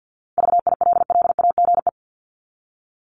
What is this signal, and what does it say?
Morse code, heard as a keyed tone, about 50 wpm, 700 Hz. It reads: VILLAGE
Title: Top 1751-2000 Words 50wpm